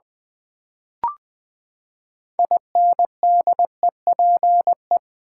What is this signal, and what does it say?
Morse code, heard as a keyed tone, about 20 wpm, 700 Hz. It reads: INDEPE
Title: Build A Word - 6 Letters At A Time - 12-Letter Words - 20wpm